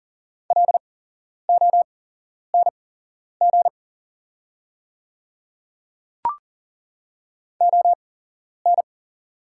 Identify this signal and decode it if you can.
Morse code, heard as a keyed tone, about 40 wpm, spaced as for 11 wpm, 700 Hz. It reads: LONG ON
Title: 8x Character Spacing - Top 200 Words - 40wpm